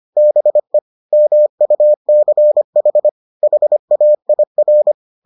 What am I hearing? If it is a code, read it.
Morse code, heard as a keyed tone, about 25 wpm, 600 Hz. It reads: BE MUCH HAIR